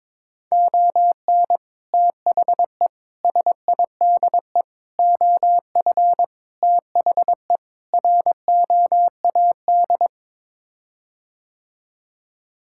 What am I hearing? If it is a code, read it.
Morse code, heard as a keyed tone, about 22 wpm, 700 Hz. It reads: ON THE SIDE OF THE ROAD
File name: practice / 22wpm / Sets of 6 Words - 22wpm